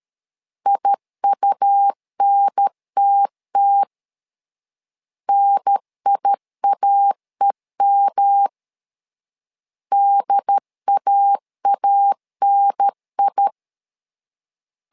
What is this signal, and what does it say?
Morse code, heard as a keyed tone, about 13 wpm, 800 Hz. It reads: IUNTT NIAEM DAANI
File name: LEZIONI / 3 MWRO / 3 M 12wpm 1541